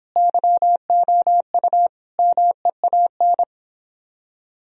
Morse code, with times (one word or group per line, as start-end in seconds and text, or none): 0.2-1.9 YOU
2.2-3.4 MEAN